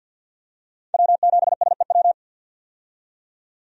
Morse code, heard as a keyed tone, about 46 wpm, 700 Hz. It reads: W7SEW
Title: USA Call Signs - 5 Characters - 50wpm